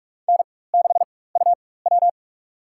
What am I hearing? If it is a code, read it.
Morse code, heard as a keyed tone, about 44 wpm, spaced as for 21 wpm, 700 Hz. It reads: N6UW
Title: USA Call Signs - 4 Characters - 4x Character Spacing - 45wpm